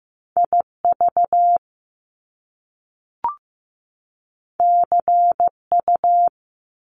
Morse code, heard as a keyed tone, about 15 wpm, 700 Hz. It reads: IV CU
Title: Top Two Letter Combinations in Words 15wpm